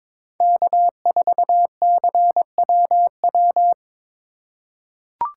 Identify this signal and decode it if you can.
Morse code, heard as a keyed tone, about 21 wpm, 700 Hz. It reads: K4CWW
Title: USA Call Signs - 5 Characters - 4x Character Spacing - 22wpm